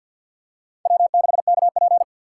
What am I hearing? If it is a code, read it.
Morse code, heard as a keyed tone, about 52 wpm, 700 Hz. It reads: W6KP
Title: USA Call Signs - 4 Characters - 6x Character Spacing - 50wpm